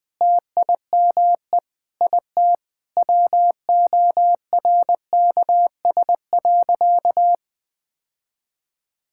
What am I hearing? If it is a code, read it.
Morse code, heard as a keyed tone, about 20 wpm, 700 Hz. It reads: TIME IT WORKS.
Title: Sentences with 3-Word Phrases - 20wpm